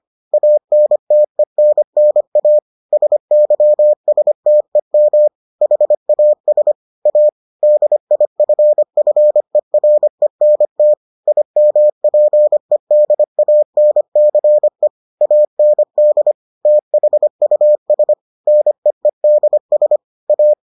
Morse code, heard as a keyed tone, about 25 wpm, 600 Hz. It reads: ANTENNA SYSTEM HAS A DIFFERENT IMPEDANCE AND THUS NEEDS A